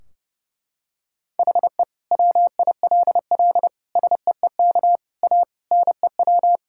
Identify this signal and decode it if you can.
Morse code, heard as a keyed tone, about 32 wpm, 700 Hz. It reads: HE WILL SEEK A NEW
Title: Sentences from Top 1600 Words 30wpm